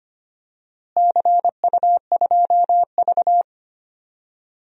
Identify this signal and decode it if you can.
Morse code, heard as a keyed tone, about 25 wpm, 700 Hz. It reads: CU2V
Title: DXCC-100 Call Signs - 4 Characters - Repeated 2x - 25wpm